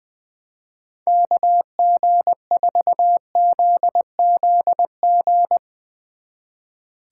Morse code, heard as a keyed tone, about 21 wpm, 700 Hz. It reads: KG4ZZG
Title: Speed Racing - USA 2x3 Call Signs 20wpm